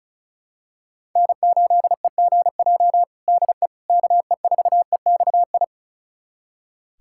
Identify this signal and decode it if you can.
Morse code, heard as a keyed tone, about 34 wpm, 700 Hz. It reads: N8EGJ DE KE4EXI
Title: USA Callsign DE Callsign 35wpm